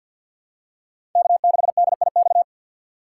Morse code, heard as a keyed tone, about 51 wpm, 700 Hz. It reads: K6DIX
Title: USA Call Signs - 5 Characters - 4x Character Spacing - 50wpm